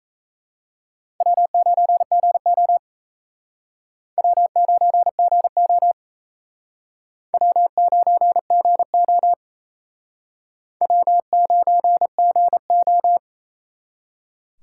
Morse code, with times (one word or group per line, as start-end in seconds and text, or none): 1.2-2.8 W9GO
4.2-5.9 W9GO
7.3-9.3 W9GO
10.8-13.2 W9GO